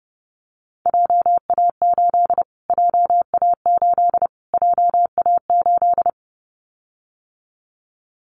Morse code, heard as a keyed tone, about 30 wpm, 700 Hz. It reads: JA8 JA8 JA8
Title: Callsign Prefixes for Top 25 DXCC - Repeated 3x - 30wpm